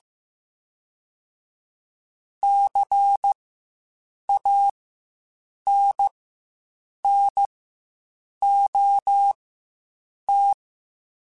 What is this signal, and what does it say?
Morse code, heard as a keyed tone, about 15 wpm, spaced as for 7 wpm, 800 Hz. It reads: CANNOT
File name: Quotes Spaced / Quotes Spaced-mp3s / Advice sp @10